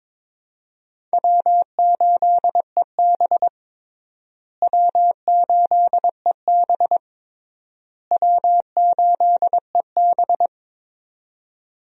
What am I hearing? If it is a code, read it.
Morse code, heard as a keyed tone, about 22 wpm, 700 Hz. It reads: W8EB W8EB W8EB
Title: USA Call Signs - 4 Characters - Repeated 3x - 22wpm